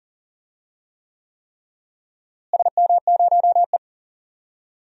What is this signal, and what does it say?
Morse code, heard as a keyed tone, about 40 wpm, 700 Hz. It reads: SM0E
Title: DXCC-100 Call Signs - 4 Characters - Repeated 3x - 40wpm